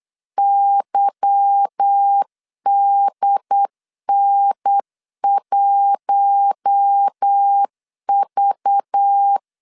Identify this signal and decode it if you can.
Morse code, heard as a keyed tone, about 8 wpm, 800 Hz. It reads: YDN1V
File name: LEZIONI / 8 YJ45 / 8 YJ 8wpm 3313